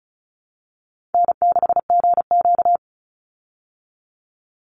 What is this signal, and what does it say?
Morse code, heard as a keyed tone, about 34 wpm, 700 Hz. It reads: N6GQ
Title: Easy Call Signs Repeated 3x - 35wpm